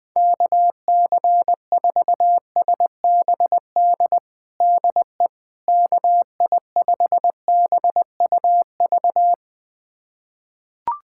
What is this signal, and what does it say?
Morse code, heard as a keyed tone, about 20 wpm, 700 Hz. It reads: KC4SBD DE KI5BUV
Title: USA Callsign DE Callsign - 6 Characters - 20wpm